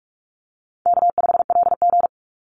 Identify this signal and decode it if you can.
Morse code, heard as a keyed tone, about 44 wpm, 700 Hz. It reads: K5LG
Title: Call Signs - 45wpm